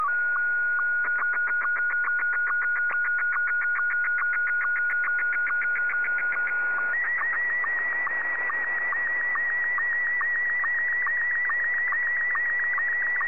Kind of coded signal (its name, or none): SSTV (slow-scan television)